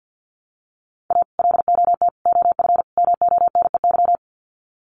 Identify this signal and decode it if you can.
Morse code, heard as a keyed tone, about 50 wpm, 700 Hz. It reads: A LOT OF MONEY
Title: Sets of 4 Words 50wpm